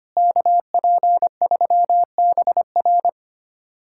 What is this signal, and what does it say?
Morse code, heard as a keyed tone, about 25 wpm, 700 Hz. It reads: KP3BR
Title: USA Callsign DE Callsign - 5 Characters - 25wpm